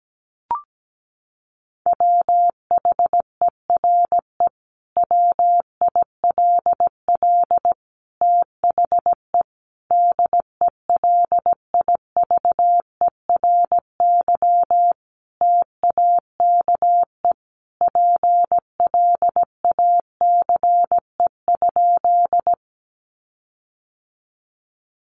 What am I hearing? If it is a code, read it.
Morse code, heard as a keyed tone, about 17 wpm, 700 Hz. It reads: WHERE WILL THE DELIVERY TAKE PLACE?
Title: Sentences from Top 1500 Words 17wpm